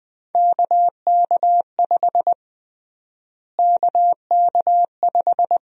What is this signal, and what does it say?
Morse code, heard as a keyed tone, about 20 wpm, 700 Hz. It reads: KK5 KK5
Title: USA Call Sign Prefixes - Repeated 2x - 20wpm